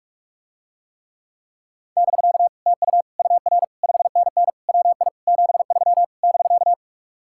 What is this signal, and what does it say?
Morse code, heard as a keyed tone, about 49 wpm, 700 Hz. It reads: <BK> TU UR 5NN WI 73 <BK>